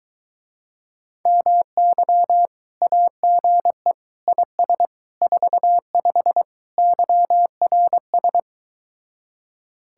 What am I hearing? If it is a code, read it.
Morse code, heard as a keyed tone, about 23 wpm, 700 Hz. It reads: MY AGE IS 45 YRS